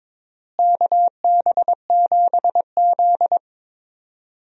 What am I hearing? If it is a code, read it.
Morse code, heard as a keyed tone, about 22 wpm, 700 Hz. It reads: KB7Z
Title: USA Call Signs - 4 Characters - Repeated 2x - 22wpm